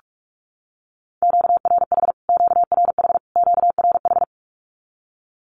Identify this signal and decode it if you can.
Morse code, heard as a keyed tone, about 44 wpm, 700 Hz. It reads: QRH QRH QRH
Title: Q-Codes Repeated 3x 45wpm